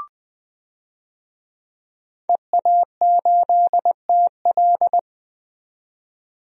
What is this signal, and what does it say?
Morse code, heard as a keyed tone, about 20 wpm, 700 Hz. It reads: EA8TL
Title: DXCC-100 Call Signs - 5 Characters - 20wpm